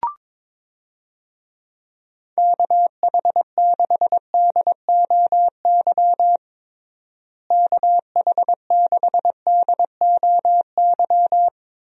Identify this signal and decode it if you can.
Morse code, heard as a keyed tone, about 22 wpm, 700 Hz. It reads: KH6DOY KH6DOY
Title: DXCC-100 Call Signs - 6 Characters - Repeated 2x - 22wpm